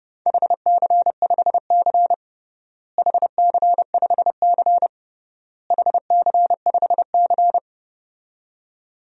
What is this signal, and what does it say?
Morse code, heard as a keyed tone, about 32 wpm, 700 Hz. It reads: HC5C HC5C HC5C
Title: DXCC-100 Call Signs - 4 Characters - Repeated 3x - 30wpm